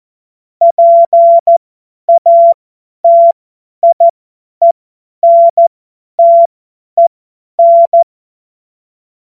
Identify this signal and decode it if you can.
Morse code, heard as a keyed tone, about 14 wpm, spaced as for 10 wpm, 680 Hz. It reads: PATIENTEN
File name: Morsekurs / Witze / Witz 19 März 2024 50 F70